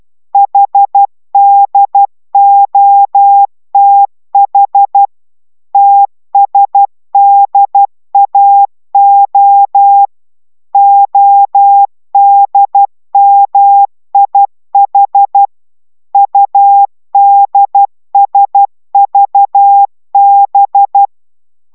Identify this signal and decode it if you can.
Morse code, heard as a keyed tone, about 12 wpm, 800 Hz. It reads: HDOTH TSDAO ODMIH UDSVB